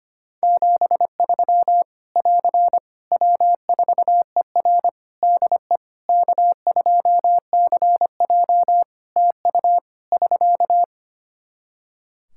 Morse code, heard as a keyed tone, about 25 wpm, 700 Hz. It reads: 73 <AR> W4ER DE K2CJ TU <SK>
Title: Practice QSO - Exchange 4 - Ending - 25wpm